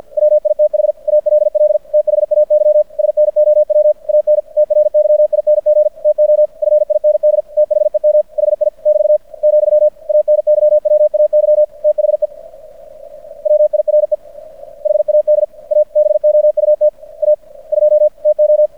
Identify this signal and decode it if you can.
Morse code, heard as a keyed tone, about 50 wpm, 600 Hz. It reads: WITH ALL THAY SNOW AN TRYING TO FIND THEM HI <BT> , ANYWAY THE WIRE HAD A BOUT A 2 TO
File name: archived / CRSnet